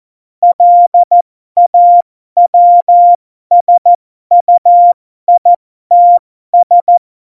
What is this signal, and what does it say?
Morse code, heard as a keyed tone, about 14 wpm, spaced as for 12 wpm, 700 Hz. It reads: LAWSUITS
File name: Morsekurs / Witze